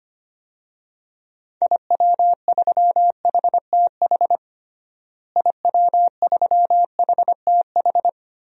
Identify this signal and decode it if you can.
Morse code, heard as a keyed tone, about 25 wpm, 700 Hz. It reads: IW3HTH IW3HTH
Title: DXCC-100 Call Signs - 6 Characters - Repeated 3x - 25wpm